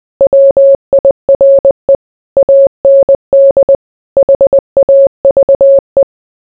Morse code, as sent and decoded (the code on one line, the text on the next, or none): .-- .. .-. / .- -. -.. / .... .- ...- .
WIRE AND HAVE